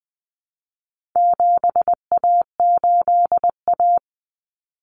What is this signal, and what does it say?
Morse code, heard as a keyed tone, about 20 wpm, 700 Hz. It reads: 7A8A
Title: Common Cut Numbers - 4-Digit 20wpm